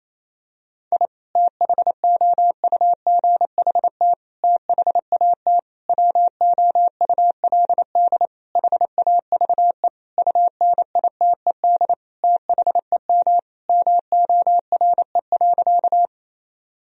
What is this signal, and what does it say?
Morse code, heard as a keyed tone, about 27 wpm, 700 Hz. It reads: I THOUGHT THAT WOULD HAVE UNITED THEM MORE.
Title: Sentences from Top 1000 Words - Review Part 4 - 28wpm